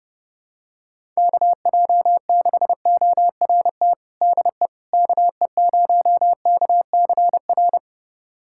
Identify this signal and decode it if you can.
Morse code, heard as a keyed tone, about 30 wpm, 700 Hz. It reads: KJ6ORT DE KE0KCR
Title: USA Callsign DE Callsign - 6 Characters - 30wpm